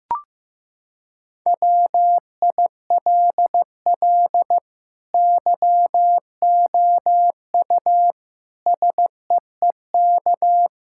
Morse code, heard as a keyed tone, about 15 wpm, 700 Hz. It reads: WILL YOU SEEK
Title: Sentences from Top 1600 Words 15wpm